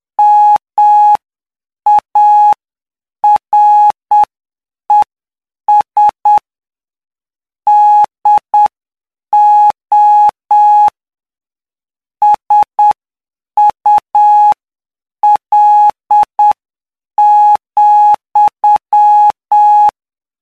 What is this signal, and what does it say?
Morse code, heard as a keyed tone, about 8 wpm, spaced as for 7 wpm, 840 Hz. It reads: MARES DO SUL,
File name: cw / py2lcd / 13-PY2LCD-Li��o 22